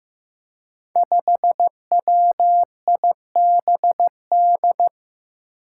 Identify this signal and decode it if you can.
Morse code, heard as a keyed tone, about 15 wpm, 700 Hz. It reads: 5WIBD
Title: Five Challenging Letter-Number Combinations 15wpm